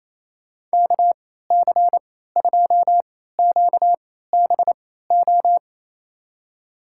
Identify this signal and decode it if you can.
Morse code, heard as a keyed tone, about 28 wpm, spaced as for 16 wpm, 700 Hz. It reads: KC2QBO
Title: USA Call Signs - 6 Characters - 3x Character Spacing - 28wpm